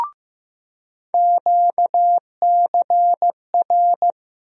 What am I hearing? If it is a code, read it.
Morse code, heard as a keyed tone, about 15 wpm, 700 Hz. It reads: QCR
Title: USA Call Sign Postfixes - 3 Letters 15wpm